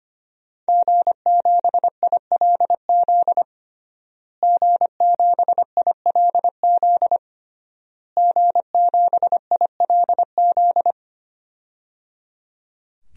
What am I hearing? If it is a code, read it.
Morse code, heard as a keyed tone, about 25 wpm, 700 Hz. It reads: G7ILZ G7ILZ G7ILZ
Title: DXCC-100 Call Signs - 5 Characters - Repeated 3x - 25wpm